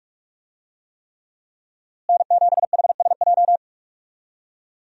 Morse code, heard as a keyed tone, about 43 wpm, 700 Hz. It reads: N7HSJ
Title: USA Call Signs - 5 Characters - Repeated 3x - 45wpm